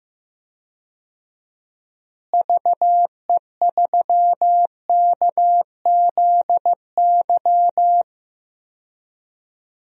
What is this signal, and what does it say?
Morse code, heard as a keyed tone, about 15 wpm, 700 Hz. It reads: VE3KZY
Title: DXCC-100 Call Signs - 6 Characters - Repeated 2x - 15wpm